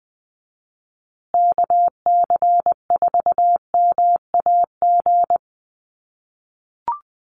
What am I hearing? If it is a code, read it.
Morse code, heard as a keyed tone, about 20 wpm, 700 Hz. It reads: KC4MAG
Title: USA Call Signs Repeated 3x 20wpm